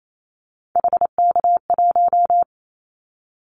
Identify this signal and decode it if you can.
Morse code, heard as a keyed tone, about 28 wpm, 700 Hz. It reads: HK1
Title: Callsign Prefixes for Top 100 DXCC - 28wpm